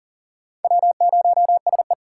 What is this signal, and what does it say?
Morse code, heard as a keyed tone, about 40 wpm, 700 Hz. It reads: W0SE
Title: USA Call Signs - 2x Character Spacing - 40wpm